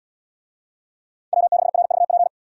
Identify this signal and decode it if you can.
Morse code, heard as a keyed tone, about 69 wpm, 700 Hz. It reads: W6RVP